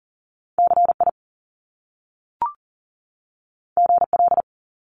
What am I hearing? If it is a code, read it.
Morse code, heard as a keyed tone, about 40 wpm, 700 Hz. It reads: CI GL
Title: Top Two Letter Combinations in Words 40wpm